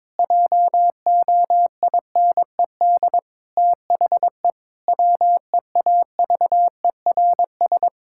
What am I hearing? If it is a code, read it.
Morse code, heard as a keyed tone, about 22 wpm, 700 Hz. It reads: JOINED THE WEAVERS
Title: Pangram Challenge v2 - 22wpm